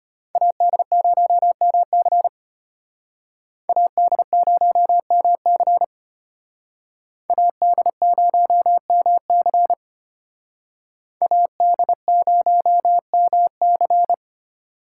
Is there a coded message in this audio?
Yes: Morse code from 0.4 to 14.1 s, heard as a keyed tone, about 32 wpm, 700 Hz. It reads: AD0MC AD0MC AD0MC AD0MC